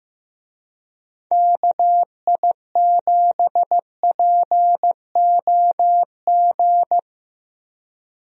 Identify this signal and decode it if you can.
Morse code, heard as a keyed tone, about 16 wpm, 700 Hz. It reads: KI7POG
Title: USA Call Signs - 6 Characters - Repeated 3x - 15wpm